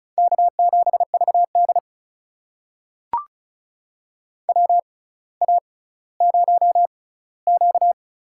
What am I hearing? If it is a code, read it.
Morse code, heard as a keyed tone, about 36 wpm, spaced as for 18 wpm, 700 Hz. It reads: K7VD WA0Q
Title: USA Call Signs - 4 Characters - 6x Character Spacing - 35wpm